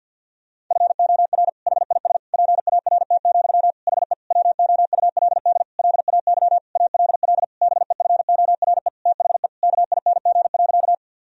Morse code, heard as a keyed tone, about 52 wpm, 700 Hz. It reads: FOR HIS PART, HE WOULD LAY ALL BEFORE THE KING.
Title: Sentences from Top 1000 Words - Review Part 2 - 50wpm